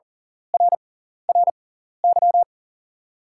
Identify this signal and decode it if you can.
Morse code, heard as a keyed tone, about 40 wpm, spaced as for 14 wpm, 700 Hz. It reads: RRY
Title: 6x Character Spacing - Top 300 Words - 40wpm